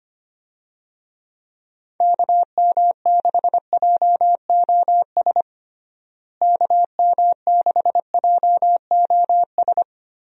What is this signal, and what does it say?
Morse code, heard as a keyed tone, about 25 wpm, 700 Hz. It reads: KM6JOS KM6JOS
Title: USA Call Signs - 6 Characters - Repeated 3x - 25wpm